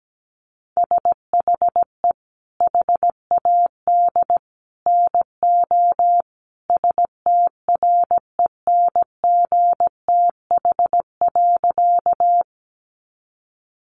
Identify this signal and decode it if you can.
Morse code, heard as a keyed tone, about 17 wpm, 700 Hz. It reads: SHE HAD NO STRENGTH.
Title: Sentences from Top 1300 Words 17wpm